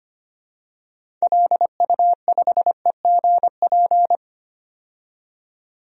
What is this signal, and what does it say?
Morse code, heard as a keyed tone, about 25 wpm, 700 Hz. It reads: LU5EGP
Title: DXCC-100 Call Signs - 6 Characters - 25wpm